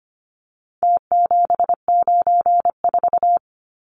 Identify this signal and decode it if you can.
Morse code, heard as a keyed tone, about 25 wpm, 700 Hz. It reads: T794
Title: Numbers - Contest Style - 4-Digit 25wpm